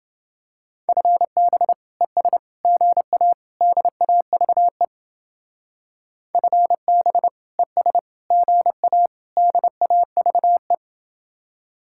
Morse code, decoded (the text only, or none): FB ES GA DAVE FB ES GA DAVE